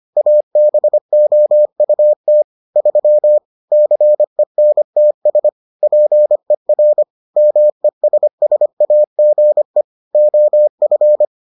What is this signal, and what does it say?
Morse code, heard as a keyed tone, about 25 wpm, 600 Hz. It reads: ABOUT 3 CENTS PER MESSAGE OF